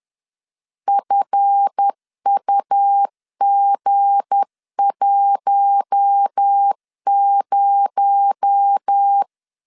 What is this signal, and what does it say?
Morse code, heard as a keyed tone, about 10 wpm, 800 Hz. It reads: FUG10
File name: LEZIONI / 10 SIGNS / 10B INT 10wpm 2844